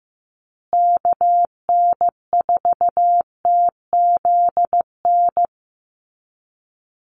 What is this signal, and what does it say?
Morse code, heard as a keyed tone, about 15 wpm, 700 Hz. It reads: KN4TZN
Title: USA Call Signs Repeated 2x 15wpm